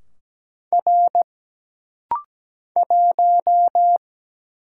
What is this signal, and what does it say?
Morse code, heard as a keyed tone, about 17 wpm, 700 Hz. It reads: R 1
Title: Single Letter-Number - Rapid-Fire 17wpm